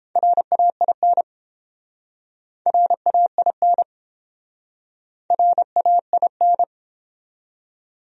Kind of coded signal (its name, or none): Morse code